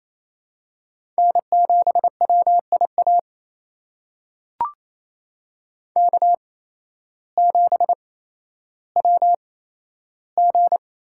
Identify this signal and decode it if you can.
Morse code, heard as a keyed tone, about 28 wpm, spaced as for 13 wpm, 700 Hz. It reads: N7WIA K7WG
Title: USA Call Signs - 5 Characters - 8x Character Spacing - 28wpm